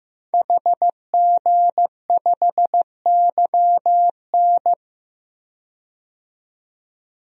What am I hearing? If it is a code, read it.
Morse code, heard as a keyed tone, about 15 wpm, 700 Hz. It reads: HG5YN
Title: DXCC-100 Call Signs - 5 Characters - 15wpm